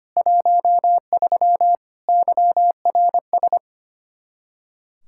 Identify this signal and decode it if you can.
Morse code, heard as a keyed tone, about 25 wpm, 700 Hz. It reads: 13 YRS
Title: Practice QSO - Exchange 3 - More Info - 25wpm